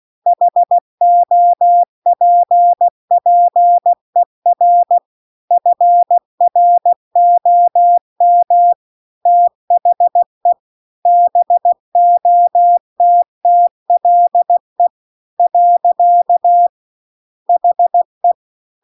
Morse code, heard as a keyed tone, about 16 wpm, 700 Hz. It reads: HOPPER FROM THE BOTTLE . HE
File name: CW-The-Big-Twoheaded-River-2-16WPM